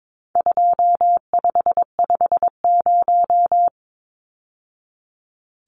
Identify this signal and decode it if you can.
Morse code, heard as a keyed tone, about 22 wpm, 700 Hz. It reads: 2550
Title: Numbers - 4-Digit 22wpm